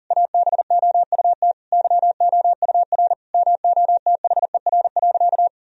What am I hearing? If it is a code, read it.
Morse code, heard as a keyed tone, about 40 wpm, 700 Hz. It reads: ABOUT YOUR MOTHER.
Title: Sentences from Top 1000 Words - Review Part 3 - 40wpm